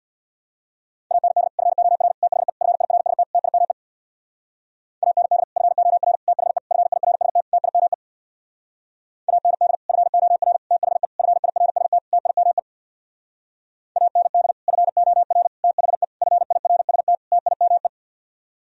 Morse code, heard as a keyed tone, about 64 wpm, 690 Hz. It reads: AND FOR THE FIRST TIME AND FOR THE FIRST TIME AND FOR THE FIRST TIME AND FOR THE FIRST TIME